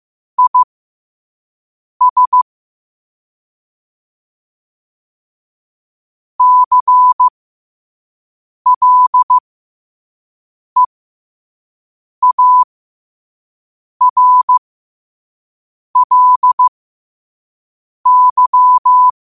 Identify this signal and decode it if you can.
Morse code, heard as a keyed tone, about 15 wpm, spaced as for 5 wpm, 1000 Hz. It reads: IS CLEARLY
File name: qotdfiles / 5wpm